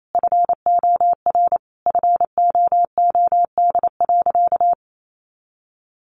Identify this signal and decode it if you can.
Morse code, heard as a keyed tone, about 28 wpm, 700 Hz. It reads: FOR FOOD.
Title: A Sound Of Thunder 28wpm